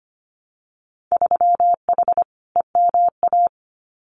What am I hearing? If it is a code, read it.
Morse code, heard as a keyed tone, about 25 wpm, 690 Hz. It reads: 3H EMA
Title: Winter Field Day Exchanges Encore 25wpm